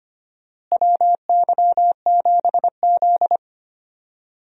Morse code, heard as a keyed tone, about 25 wpm, 700 Hz. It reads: WY7Z
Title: USA Call Signs - 4 Characters - Repeated 2x - 25wpm